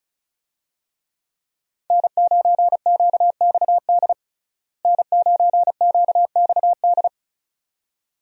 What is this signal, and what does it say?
Morse code, heard as a keyed tone, about 34 wpm, 700 Hz. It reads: N9QXD N9QXD
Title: USA Call Signs - 5 Characters - Repeated 2x - 35wpm